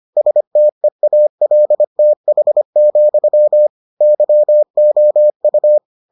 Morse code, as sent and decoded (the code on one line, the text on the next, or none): ... - . .- .-.. - .... --..-- / -.-- --- ..-
STEALTH, YOU